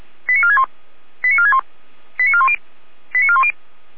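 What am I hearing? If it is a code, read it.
Five-tone selective call (ZVEI), heard as quick runs of five tones, tones of about 70 ms. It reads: 89461 89461 89410 89410
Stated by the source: ZVEI five-tone selective call 89108 received via SDR by feuerwehr-orlamuende.org (Feuerwehr Orlamünde fire brigade).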